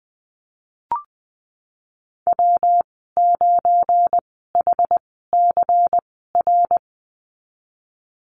Morse code, heard as a keyed tone, about 20 wpm, spaced as for 15 wpm, 700 Hz. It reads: W9HCR
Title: 213 Lesson 34g - Number 8 - Call Signs 20wpm